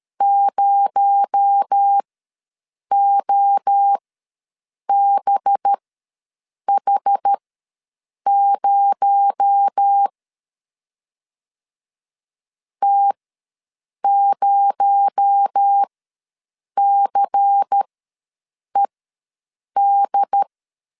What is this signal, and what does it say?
Morse code, heard as a keyed tone, about 13 wpm, spaced as for 7 wpm, 800 Hz. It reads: 0OBH0 T0CED
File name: LEZIONI / 6 PKX0 / 6 PKX0 ws 12wpm 3448